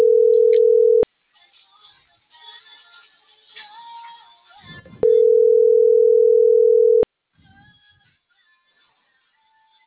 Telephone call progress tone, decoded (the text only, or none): ringback tone